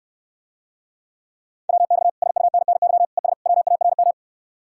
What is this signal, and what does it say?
Morse code, heard as a keyed tone, about 69 wpm, 700 Hz. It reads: WX SUNNY ES WARM